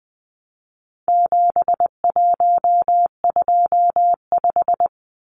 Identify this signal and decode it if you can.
Morse code, heard as a keyed tone, about 20 wpm, 700 Hz. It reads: 7125
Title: Numbers - 4-Digit 20wpm